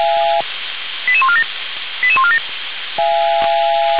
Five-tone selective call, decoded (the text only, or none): 88147 88147